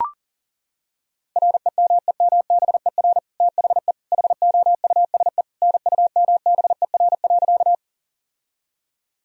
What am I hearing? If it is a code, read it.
Morse code, heard as a keyed tone, about 37 wpm, 700 Hz. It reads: REMEMBER THE HOUSE NUMBER.
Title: Sentences from Top 1000 Words - Review Part 1 - 40wpm